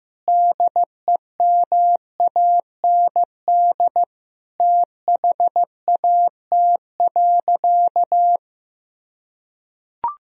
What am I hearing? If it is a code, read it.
Morse code, heard as a keyed tone, about 15 wpm, 700 Hz. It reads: DEMAND THAT.
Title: Sentences from Top 1000 Words - Review Part 4 - 15wpm